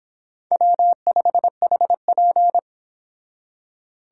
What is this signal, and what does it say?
Morse code, heard as a keyed tone, about 31 wpm, 700 Hz. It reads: W5HP